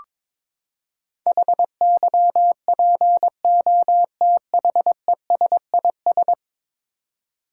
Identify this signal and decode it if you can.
Morse code, heard as a keyed tone, about 22 wpm, 700 Hz. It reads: HYPOTHESIS
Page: morse-code-ninja-practice-22wpm